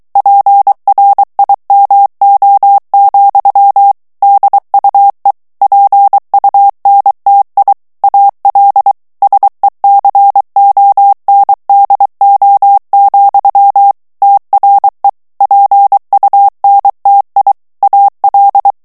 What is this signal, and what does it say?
Morse code, heard as a keyed tone, about 23 wpm, 800 Hz. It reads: PRIMO, DUE PUNTI AL SECONDO, TRE PUNTI AL
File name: ITALIEN texte clair 2 vitesse 22 mots minute